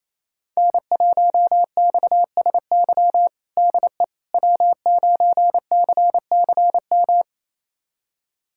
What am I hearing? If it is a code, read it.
Morse code, heard as a keyed tone, about 30 wpm, 700 Hz. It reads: N1XSY DE W9CCM
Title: USA Callsign DE Callsign - 5 Characters - 28wpm